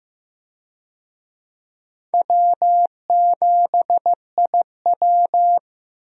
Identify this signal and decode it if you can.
Morse code, heard as a keyed tone, about 15 wpm, 700 Hz. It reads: W7IW
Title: USA Call Signs - 4 Characters - 15wpm